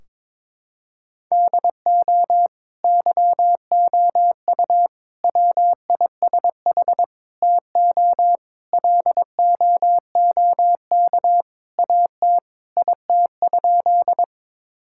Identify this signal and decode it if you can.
Morse code, heard as a keyed tone, about 22 wpm, 700 Hz. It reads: DO YOU WISH TO LOOK AT IT?
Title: Sentences from Top 1600 Words 22wpm